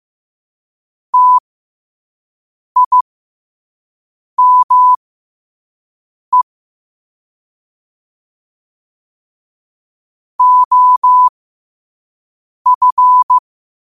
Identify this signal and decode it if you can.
Morse code, heard as a keyed tone, about 15 wpm, spaced as for 5 wpm, 1000 Hz. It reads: TIME OF
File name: qotdfiles / 5wpm